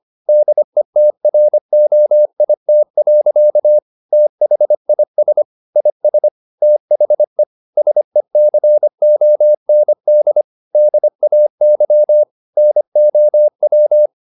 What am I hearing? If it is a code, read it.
Morse code, heard as a keyed tone, about 25 wpm, 600 Hz. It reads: DETROIT. THIS IS THE SECOND DAY NOW
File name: the old man and the sea part 1 25wpm